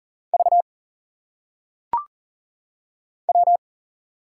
Morse code, heard as a keyed tone, about 38 wpm, 700 Hz. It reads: V W
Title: Single Letters - Phonetic Pronunciation 40wpm